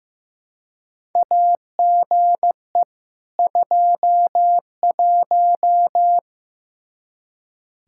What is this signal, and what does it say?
Morse code, heard as a keyed tone, about 15 wpm, 700 Hz. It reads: AGE 21
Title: Practice QSO - Exchange 3 - More Info - 15wpm